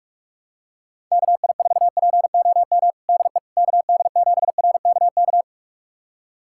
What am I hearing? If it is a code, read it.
Morse code, heard as a keyed tone, about 42 wpm, 700 Hz. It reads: KI4POM DE KD7RKK
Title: USA Callsign DE Callsign - 6 Characters - 45wpm